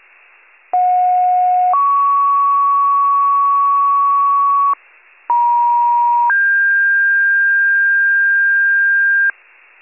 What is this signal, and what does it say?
Two-tone page, heard as a one-second tone, then a three-second tone, about 730 then 1120 Hz, about 950 then 1600 Hz, with noise. It is heard from 0.7 to 9.3 s.